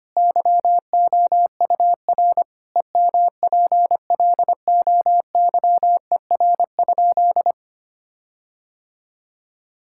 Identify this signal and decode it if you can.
Morse code, heard as a keyed tone, about 25 wpm, 700 Hz. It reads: YOUR EMPLOYER?
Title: Sentences from Top 1600 Words 25wpm